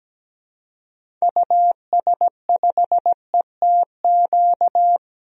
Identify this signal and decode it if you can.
Morse code, heard as a keyed tone, about 17 wpm, 700 Hz. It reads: US5ETQ
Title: DXCC-100 Call Signs - 6 Characters - Repeated 3x - 17wpm